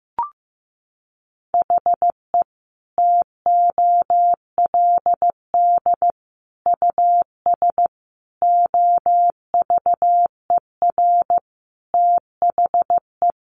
Morse code, heard as a keyed tone, about 15 wpm, 700 Hz. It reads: HE TOLD US OVER THE
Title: Sentences from Top 1100 Words 15wpm